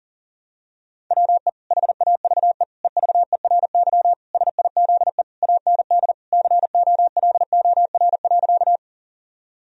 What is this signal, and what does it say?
Morse code, heard as a keyed tone, about 40 wpm, 700 Hz. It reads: WE HAVE EVERY SIZE AND COLOR.
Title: Sentences from Top 1000 Words - Review Part 2 - 40wpm